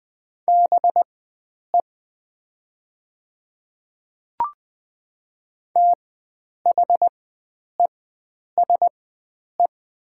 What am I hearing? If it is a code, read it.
Morse code, heard as a keyed tone, about 20 wpm, spaced as for 9 wpm, 700 Hz. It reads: BE THESE
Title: 4x Character Spacing - Top 100 Words - 20wpm